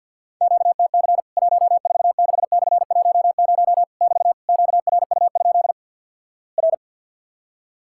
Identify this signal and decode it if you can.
Morse code, heard as a keyed tone, about 50 wpm, 700 Hz. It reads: QTC 146/10 <BT> QRU? R
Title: Worked All Europe DX Contest - QTC Exchange - 50wpm